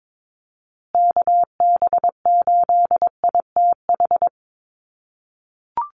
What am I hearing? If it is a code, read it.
Morse code, heard as a keyed tone, about 22 wpm, 700 Hz. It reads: KB8ITH
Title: USA Call Signs Repeated 2x 22wpm